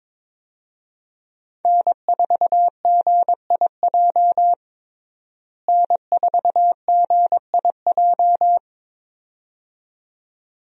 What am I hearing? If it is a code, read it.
Morse code, heard as a keyed tone, about 22 wpm, 700 Hz. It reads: N4GIJ N4GIJ
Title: USA Call Signs - 5 Characters - Repeated 2x - 22wpm